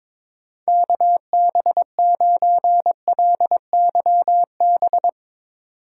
Morse code, heard as a keyed tone, about 22 wpm, 700 Hz. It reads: KB9LYB
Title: USA Call Signs - 6x Character Spacing - 22wpm